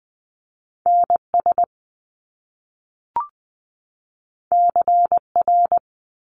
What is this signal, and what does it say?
Morse code, heard as a keyed tone, about 20 wpm, 700 Hz. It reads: NS CR
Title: Top Two Letter Combinations in Words 20wpm